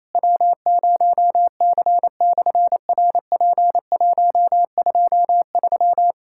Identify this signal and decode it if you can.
Morse code, heard as a keyed tone, about 27 wpm, 700 Hz. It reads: W0C/RP123
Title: SOTA QSO Practice 28wpm